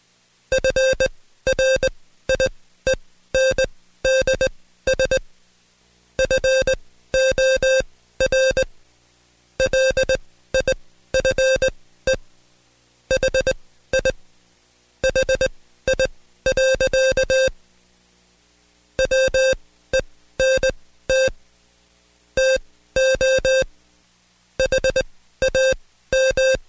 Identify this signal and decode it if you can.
Morse code, heard as a keyed tone, about 20 wpm, spaced as for 14 wpm, 530 Hz. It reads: FRIENDS FOR LIFE HI HI. WENT TO HAM
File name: Code Quick / AudioDownld2 / Seg133-Stepper6 13WPM